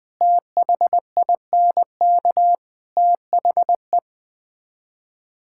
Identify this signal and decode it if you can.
Morse code, heard as a keyed tone, about 20 wpm, 700 Hz. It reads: THINK THE
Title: Build A Phrase - 4-Word Phrases - 20wpm